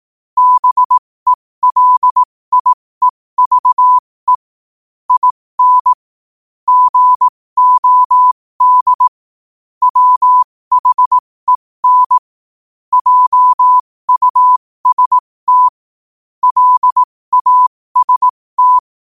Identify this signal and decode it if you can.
Morse code, heard as a keyed tone, about 18 wpm, spaced as for 15 wpm, 1000 Hz. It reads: BELIEVE IN GOD WHEN JUST LAST
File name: qotdfiles / 15wpm